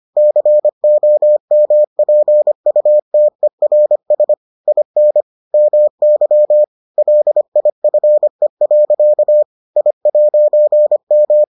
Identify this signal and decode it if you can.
Morse code, heard as a keyed tone, about 25 wpm, 600 Hz. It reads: COMPUTERS IN MY LIFE. I'M